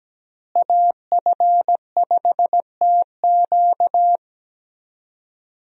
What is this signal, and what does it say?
Morse code, heard as a keyed tone, about 17 wpm, 700 Hz. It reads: AF5TQ
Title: USA Call Signs - 5 Characters - Repeated 2x - 17wpm